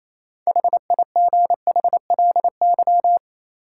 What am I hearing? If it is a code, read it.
Morse code, heard as a keyed tone, about 28 wpm, 700 Hz. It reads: HIGHLY
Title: Sentences from Top 1100 Words - New Word Drill - 28wpm